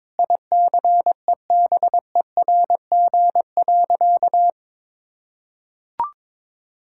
Morse code, heard as a keyed tone, about 21 wpm, 700 Hz. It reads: ICEBERG.
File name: Sentences with 7-Word Phrases - 22wpm